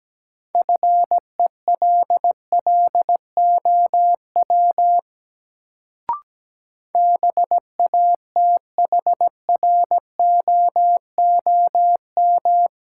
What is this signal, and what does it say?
Morse code, heard as a keyed tone, about 17 wpm, 700 Hz. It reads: FELLOW BATHROOM
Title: Sets of 5 Words Encore - New Word Drill - 17wpm